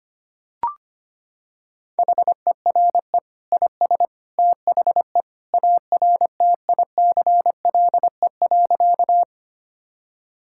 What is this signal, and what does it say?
Morse code, heard as a keyed tone, about 25 wpm, 700 Hz. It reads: HERE IS THE ARTICLE.
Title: Sentences from Top 1000 Words - Review Part 5 - 25wpm